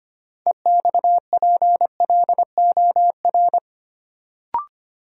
Morse code, heard as a keyed tone, about 25 wpm, 700 Hz. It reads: EXPLOR
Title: Sentences from Top 1100 Words - New Word Drill Alt - 25wpm